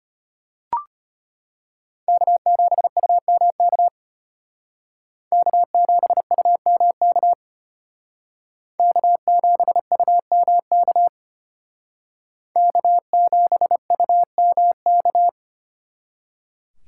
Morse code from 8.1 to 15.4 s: K7UMK K7UMK

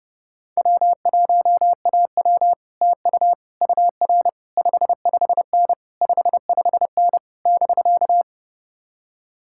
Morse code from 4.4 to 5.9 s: 55N

